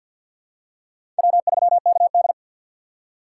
Morse code, heard as a keyed tone, about 47 wpm, 700 Hz. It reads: W3KD
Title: USA Call Signs - 4 Characters - 3x Character Spacing - 50wpm